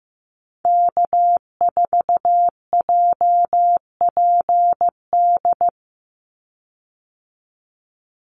Morse code, heard as a keyed tone, about 15 wpm, 700 Hz. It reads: K4JPD
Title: Call Signs Repeated 2x - 15wpm